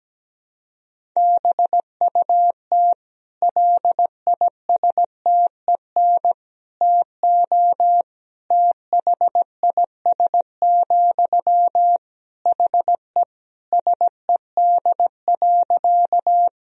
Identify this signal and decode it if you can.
Morse code, heard as a keyed tone, about 17 wpm, 700 Hz. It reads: BUT LISTEN TO THIS, HE SED.
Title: Sentences from Top 400 Words and Abbreviations 17wpm